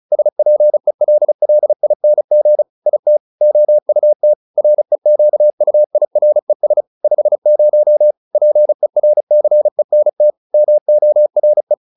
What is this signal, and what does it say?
Morse code, heard as a keyed tone, about 34 wpm, 600 Hz. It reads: SPELLING IT OUT REQUIRES 50 PERCENT MORE